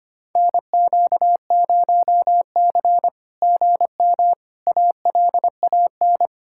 Morse code, heard as a keyed tone, about 24 wpm, 700 Hz. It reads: NQ0C GM ALAN
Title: SOTA QSO Practice 25wpm